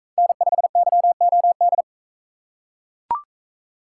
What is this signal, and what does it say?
Morse code, heard as a keyed tone, about 37 wpm, 700 Hz. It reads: N5YOD